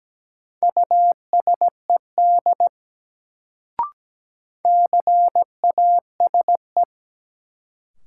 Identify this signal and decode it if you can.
Morse code, heard as a keyed tone, about 17 wpm, 700 Hz. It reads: USED CASE
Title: Sets of 2 Words - New Word Drill - 17wpm